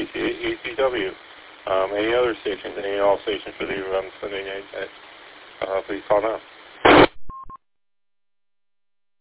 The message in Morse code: none

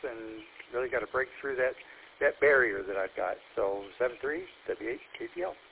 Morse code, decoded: none

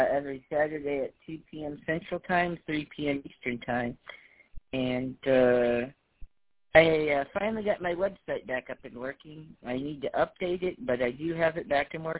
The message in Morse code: none

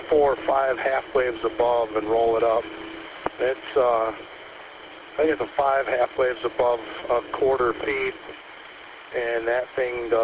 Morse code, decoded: none

